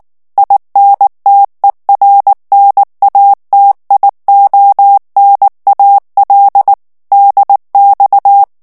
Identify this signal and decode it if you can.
Morse code, heard as a keyed tone, about 19 wpm, 800 Hz. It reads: INTERNATIONAL DX
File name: morse mp3 / ITALIEN texte clair 2 vitesse 18 mots minute